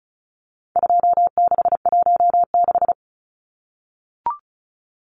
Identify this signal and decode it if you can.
Morse code, heard as a keyed tone, about 34 wpm, 700 Hz. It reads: 2616
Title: Numbers - 4-Digit 35wpm